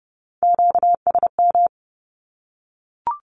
Morse code, heard as a keyed tone, about 30 wpm, 700 Hz. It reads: QSM
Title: Q-Codes 30wpm